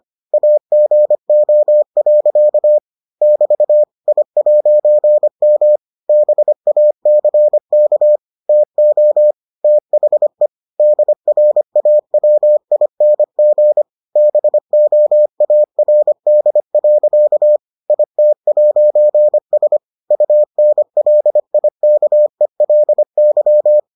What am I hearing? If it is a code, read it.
Morse code, heard as a keyed tone, about 25 wpm, 600 Hz. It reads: AGO. <BT> I'M BACK TO THE DRAWING BOARD. IT'S UNLIKELY